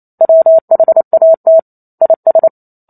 Morse code, heard as a keyed tone, about 28 wpm, 650 Hz. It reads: WHAT IS
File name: Chapter 2 - 28 WPM